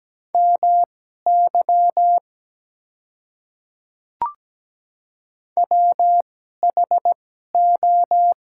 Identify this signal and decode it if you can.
Morse code, heard as a keyed tone, about 17 wpm, spaced as for 12 wpm, 700 Hz. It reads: MY WHO
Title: 2x Character Spacing - Top 200 Words - 17wpm